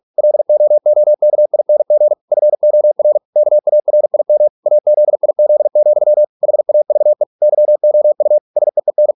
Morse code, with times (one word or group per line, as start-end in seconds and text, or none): 0.2-2.1 LOOKING
2.3-3.2 FOR
3.4-4.5 KARIM
4.7-6.2 AZIZ,
6.4-7.2 HAVE
7.4-8.4 YOU
8.6-9.1 SEEN